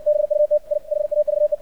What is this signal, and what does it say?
Morse code, heard as a keyed tone, about 68 wpm, 600 Hz. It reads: BUT I HAVE
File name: archived / CRSnet